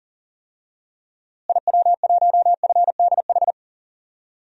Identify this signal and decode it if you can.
Morse code, heard as a keyed tone, about 40 wpm, 700 Hz. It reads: IW1FDH
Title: DXCC-100 Call Signs - 6 Characters - Repeated 3x - 40wpm